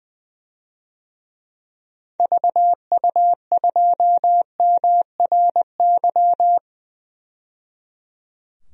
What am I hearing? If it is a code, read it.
Morse code, heard as a keyed tone, about 20 wpm, 700 Hz. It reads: VU2MRY
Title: DXCC-100 Call Signs - 6 Characters - 20wpm